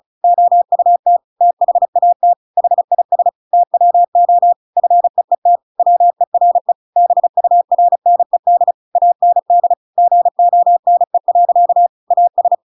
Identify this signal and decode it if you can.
Morse code, heard as a keyed tone, about 34 wpm, 700 Hz. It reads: OUT THAT HIS TWO FEET WERE BURNED AND GONE. AS
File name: Pinocchio / Pinocchio ch7 35 35 ik1hge